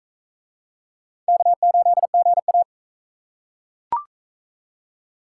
Morse code, heard as a keyed tone, about 41 wpm, 700 Hz. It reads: K8GA